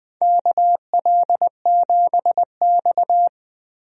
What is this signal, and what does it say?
Morse code, heard as a keyed tone, about 20 wpm, 700 Hz. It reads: KL7X
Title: DXCC-100 Call Signs - 4 Characters - 20wpm